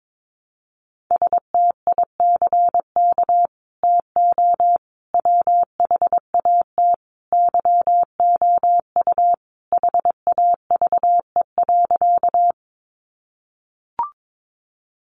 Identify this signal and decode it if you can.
Morse code, heard as a keyed tone, about 22 wpm, 700 Hz. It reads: STICK TO WHAT YOU HAVE.
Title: Sentences from Top 1200 Words 22wpm